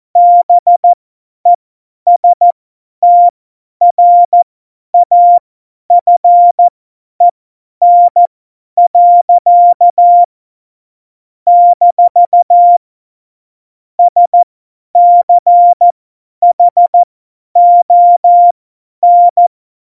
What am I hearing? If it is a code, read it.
Morse code, heard as a keyed tone, about 14 wpm, spaced as for 10 wpm, 700 Hz. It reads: BESTRAFEN. - SCHON
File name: Morsekurs / Witze / Witz 01. August 17, Tempo 50 F70